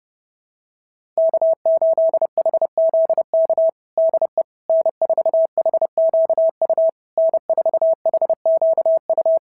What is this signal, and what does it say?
Morse code, heard as a keyed tone, about 31 wpm, 690 Hz. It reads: K8HZK DE N4HQU N4HQU
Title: SOTA QSO Practice 30wpm